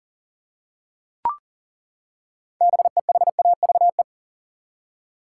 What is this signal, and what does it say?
Morse code, heard as a keyed tone, about 40 wpm, 700 Hz. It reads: BEHAVE
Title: Six Letter Words - 40wpm